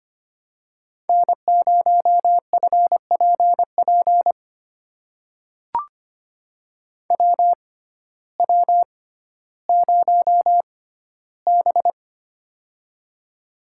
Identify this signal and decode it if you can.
Morse code, heard as a keyed tone, about 25 wpm, spaced as for 14 wpm, 700 Hz. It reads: N0FPP WW0B
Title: USA Call Signs - 6x Character Spacing - 25wpm